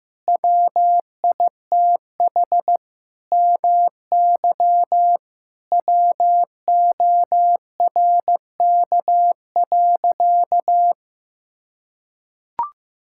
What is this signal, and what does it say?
Morse code, heard as a keyed tone, about 15 wpm, 700 Hz. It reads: WITH MY WORK.
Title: Sentences from Top 1000 Words - Review Part 6 - 15wpm